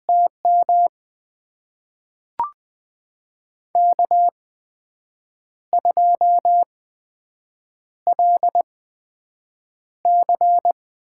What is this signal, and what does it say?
Morse code, heard as a keyed tone, about 20 wpm, spaced as for 7 wpm, 700 Hz. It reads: TM K2LC
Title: USA Call Signs - 8x Character Spacing - 20wpm